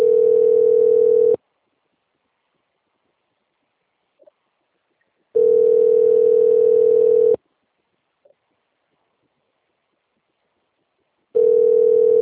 Ringback tone (North America), 440 + 480 Hz.
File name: out- - - - - .536